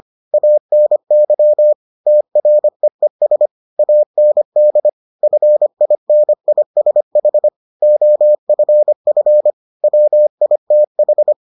Morse code, heard as a keyed tone, about 25 wpm, 600 Hz. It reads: ANY TREES AND FINISH OFF WITH